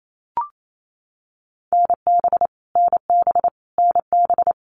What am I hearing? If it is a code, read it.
Morse code, heard as a keyed tone, about 28 wpm, 700 Hz. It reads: NB NB NB
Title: Canadian Provinces and Territories - Repeated 3x - 28wpm